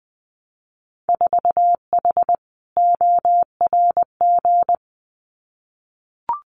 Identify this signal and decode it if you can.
Morse code, heard as a keyed tone, about 20 wpm, 700 Hz. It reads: 4H ORG
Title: Winter Field Day Exchanges Encore 20wpm